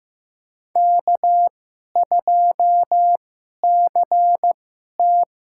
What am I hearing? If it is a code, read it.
Morse code, heard as a keyed tone, about 15 wpm, spaced as for 11 wpm, 690 Hz. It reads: K2CT
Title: USA Call Signs - 4 Characters - 2x Character Spacing - 15wpm